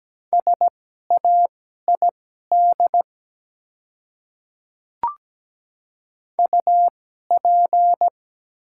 Morse code, heard as a keyed tone, about 17 wpm, spaced as for 12 wpm, 700 Hz. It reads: SAID UP
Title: 2x Character Spacing - Top 100 Words - 17wpm